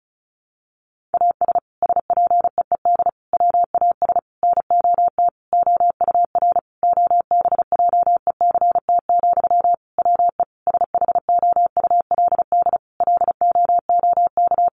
Morse code, heard as a keyed tone, about 34 wpm, 700 Hz. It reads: AS SPEED WAS NOT OUR OBJECT, WE SHOULD LOOK
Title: Sentences from Top 800 Words 35wpm